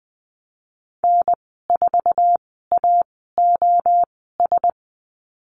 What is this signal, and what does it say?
Morse code, heard as a keyed tone, about 20 wpm, spaced as for 14 wpm, 700 Hz. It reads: N4AOS
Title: 069 Lesson 12g - Letter d - Call Signs 20wpm